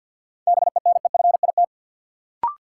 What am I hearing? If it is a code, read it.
Morse code, heard as a keyed tone, about 53 wpm, 700 Hz. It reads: BENEFIT
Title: Sentences from Top 1300 Words - New Word Drill Alt - 50wpm